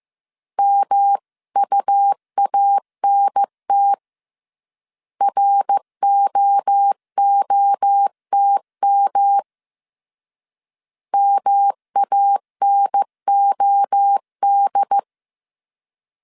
Morse code, heard as a keyed tone, about 15 wpm, 800 Hz. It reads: MUANT ROOTM MANOD